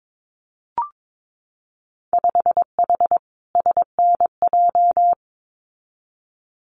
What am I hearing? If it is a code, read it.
Morse code, heard as a keyed tone, about 22 wpm, 700 Hz. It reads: 5H SNJ
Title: Winter Field Day Exchanges Encore 22wpm